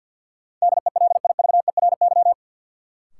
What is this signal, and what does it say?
Morse code, heard as a keyed tone, about 53 wpm, 700 Hz. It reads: DELIVERY
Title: Sentences from Top 1500 Words - New Word Drill - 50wpm